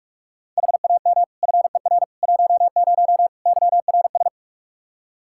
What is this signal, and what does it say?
Morse code, heard as a keyed tone, about 45 wpm, 700 Hz. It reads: HAM FER 10 YRS
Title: Practice QSO - Exchange 3 - More Info - 45wpm